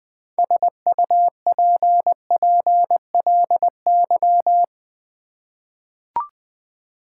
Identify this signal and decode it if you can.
Morse code, heard as a keyed tone, about 20 wpm, 700 Hz. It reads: SUPPLY
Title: Six Letter Words - 20wpm